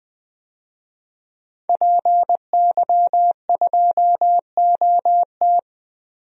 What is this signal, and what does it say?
Morse code, heard as a keyed tone, about 21 wpm, 700 Hz. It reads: PY2OT
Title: DXCC-100 Call Signs - 5 Characters - Repeated 3x - 20wpm